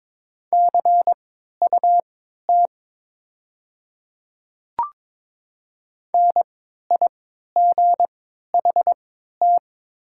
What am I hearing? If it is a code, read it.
Morse code, heard as a keyed tone, about 22 wpm, spaced as for 12 wpm, 700 Hz. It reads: CUT NIGHT
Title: 3x Character Spacing - Top 500 Words - 22wpm